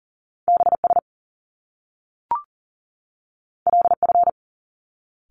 Morse code, heard as a keyed tone, about 42 wpm, 700 Hz. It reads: BS LF
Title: Top Two Letter Combinations in Words 40wpm